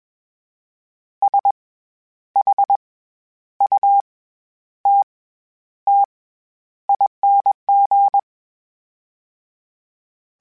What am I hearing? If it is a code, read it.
Morse code, heard as a keyed tone, about 21 wpm, spaced as for 10 wpm, 800 Hz. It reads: SHUTTING
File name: Special INGs-mp3s / Special INGs-mp3s / SlowToFaster ing @20